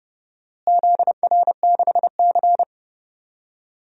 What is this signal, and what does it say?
Morse code, heard as a keyed tone, about 30 wpm, 700 Hz. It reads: ZR6C
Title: DXCC-100 Call Signs - 4 Characters - Repeated 3x - 30wpm